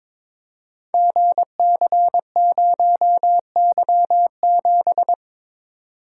Morse code, heard as a keyed tone, about 22 wpm, 700 Hz. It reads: GC0Y7
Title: Code Groups - 5 Letter:Numbers - 22wpm